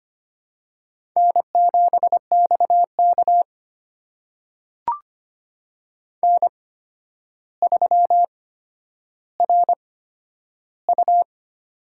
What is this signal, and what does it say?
Morse code, heard as a keyed tone, about 25 wpm, spaced as for 11 wpm, 700 Hz. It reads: N7XK N3RU
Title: USA Call Signs - 4 Characters - 8x Character Spacing - 25wpm